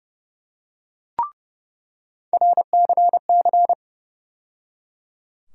Morse code, heard as a keyed tone, about 31 wpm, 700 Hz. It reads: RCC
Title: USA Call Sign Postfixes - 3 Letters 30wpm